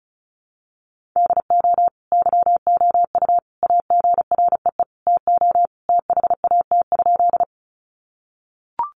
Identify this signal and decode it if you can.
Morse code, heard as a keyed tone, about 34 wpm, 700 Hz. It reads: DO YOU AGREE TO THAT?
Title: Sentences from Top 1100 Words 35wpm